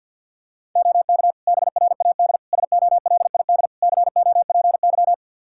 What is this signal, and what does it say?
Morse code, heard as a keyed tone, about 46 wpm, 700 Hz. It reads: OK BRAD SOLID COPY
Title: K1USN SST QSO Practice 50wpm